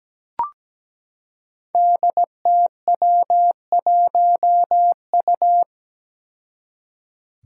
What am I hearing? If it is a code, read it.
Morse code, heard as a keyed tone, about 17 wpm, 700 Hz. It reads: DTW1U
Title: Five Challenging Letter-Number Combinations 17wpm